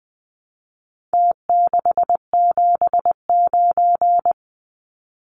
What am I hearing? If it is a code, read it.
Morse code, heard as a keyed tone, about 19 wpm, 700 Hz. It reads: T679
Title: Numbers - Contest Style - 4-Digit 20wpm